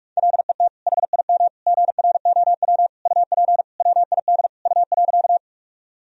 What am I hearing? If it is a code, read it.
Morse code, heard as a keyed tone, about 46 wpm, 700 Hz. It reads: LET HIM GROW UP WID U.
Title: Sentences from Top 300 Words and Abbreviations 45wpm